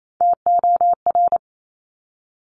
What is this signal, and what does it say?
Morse code, heard as a keyed tone, about 28 wpm, 700 Hz. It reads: TOR
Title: Top Three Letter Combinations in Words 28wpm